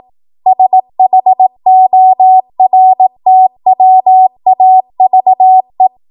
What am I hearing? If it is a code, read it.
Morse code, heard as a keyed tone, about 18 wpm, 750 Hz. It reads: SHORTWAVE
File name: Morse / 18WPM